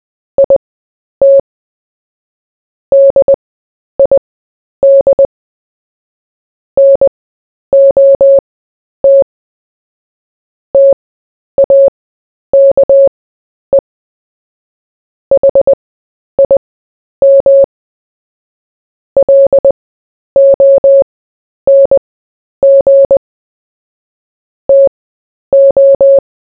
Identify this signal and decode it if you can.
Morse code, heard as a keyed tone, about 20 wpm, spaced as for 10 wpm, 550 Hz. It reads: IT DID NOT TAKE HIM LONG TO
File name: cw / fists / Short-Story-4-10wpm